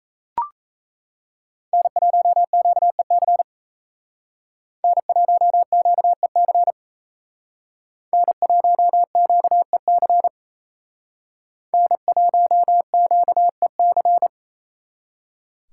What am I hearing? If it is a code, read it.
Morse code, heard as a keyed tone, about 35 wpm, 700 Hz. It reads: N1QEC N1QEC N1QEC N1QEC
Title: Speed Racing - USA Call Signs 28wpm